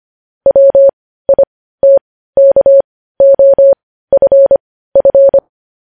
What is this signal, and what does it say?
Morse code, heard as a keyed tone, about 25 wpm, spaced as for 15 wpm, 550 Hz. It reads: WITKOFF
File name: podcast / News headlines-1x-25x15